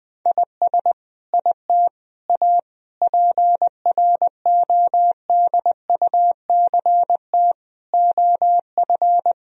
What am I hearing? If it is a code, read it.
Morse code, heard as a keyed tone, about 20 wpm, 700 Hz. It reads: IS IT A PRODUCT OF
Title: Sentences from Top 1000 Words - Review Part 2 - 20wpm